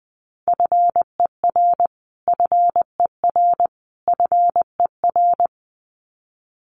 Morse code, heard as a keyed tone, about 20 wpm, 700 Hz. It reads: FER FER FER
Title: Abbreviations Repeated 3x - 20wpm